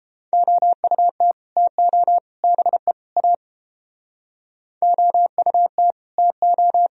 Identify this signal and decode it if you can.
Morse code, heard as a keyed tone, about 33 wpm, 700 Hz. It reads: OUT TO BE A OUT TO